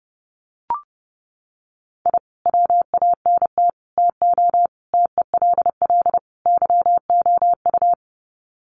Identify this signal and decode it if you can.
Morse code, heard as a keyed tone, about 30 wpm, 700 Hz. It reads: I WANT TO TELL YOU
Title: Sets of 5 Words Encore 30wpm